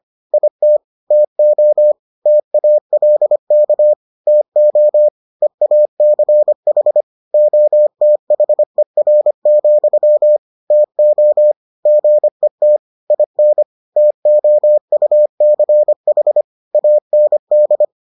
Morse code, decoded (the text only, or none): IT TO TALK TO EACH OTHER, TO GET IN TOUCH AND